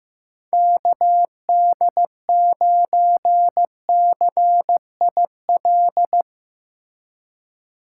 Morse code, heard as a keyed tone, about 15 wpm, 700 Hz. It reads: KD9CIL
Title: Speed Racing - USA 2x3 Call Signs 15wpm